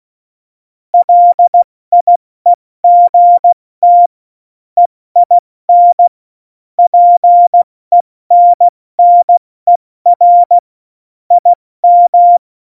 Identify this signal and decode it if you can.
Morse code, heard as a keyed tone, about 16 wpm, 700 Hz. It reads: LIEGT EIN PENNER IM
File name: Morsekurs / Witze / Witz 07. Januar 2020, 70 F80